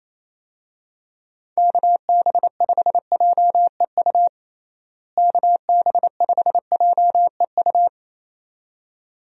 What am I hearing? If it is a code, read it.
Morse code, heard as a keyed tone, about 28 wpm, 700 Hz. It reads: KB5JEU KB5JEU
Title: USA Call Signs - 6 Characters - Repeated 2x - 28wpm